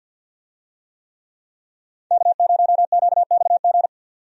Morse code, heard as a keyed tone, about 46 wpm, 700 Hz. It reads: K0QXG
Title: USA Call Signs - 5 Characters - 50wpm